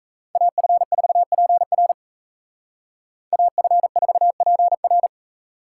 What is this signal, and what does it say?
Morse code, heard as a keyed tone, about 33 wpm, 700 Hz. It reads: AF4PR AF4PR
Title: Speed Racing - USA Call Signs 28wpm